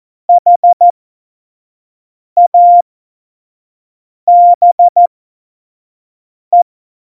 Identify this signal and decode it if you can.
Morse code, heard as a keyed tone, about 14 wpm, spaced as for 5 wpm, 700 Hz. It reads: HABE
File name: Morsekurs / Witze / Witz 10. Oktober 17, Tempo 25